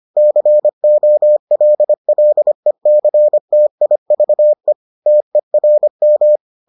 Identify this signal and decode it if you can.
Morse code, heard as a keyed tone, about 25 wpm, 600 Hz. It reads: COLLECTIVE TERM